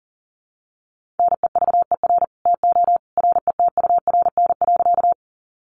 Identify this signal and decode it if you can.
Morse code, heard as a keyed tone, about 39 wpm, 700 Hz. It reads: NEVER TO RETURN.
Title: Sentences from Top 800 Words 40wpm